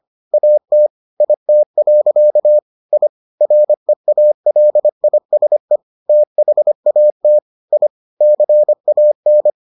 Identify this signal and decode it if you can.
Morse code, heard as a keyed tone, about 25 wpm, 600 Hz. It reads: AT IT. I REALISE THAT I CAN